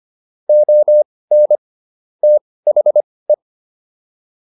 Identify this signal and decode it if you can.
Morse code, heard as a keyed tone, about 25 wpm, spaced as for 18 wpm, 600 Hz. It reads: ON THE
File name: WD104 18